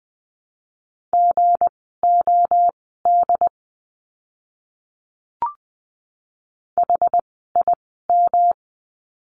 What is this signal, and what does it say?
Morse code, heard as a keyed tone, about 20 wpm, spaced as for 15 wpm, 700 Hz. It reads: GOD HIM
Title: 160 Lesson 26f - Letter g - Words 20wpm